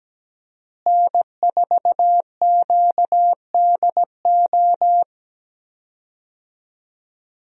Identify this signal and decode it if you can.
Morse code, heard as a keyed tone, about 17 wpm, 700 Hz. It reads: N4QDO
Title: USA Call Signs - 5 Characters - Repeated 2x - 17wpm